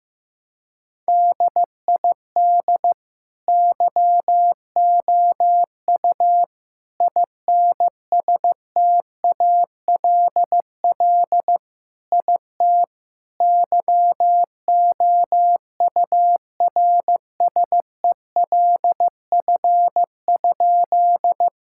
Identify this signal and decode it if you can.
Morse code, heard as a keyed tone, about 15 wpm, 700 Hz. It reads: DID YOU INSTALL IT YOURSELF?
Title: Sentences from Top 1600 Words 15wpm